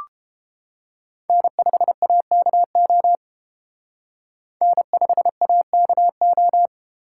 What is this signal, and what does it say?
Morse code, heard as a keyed tone, about 28 wpm, 700 Hz. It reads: N5AKO N5AKO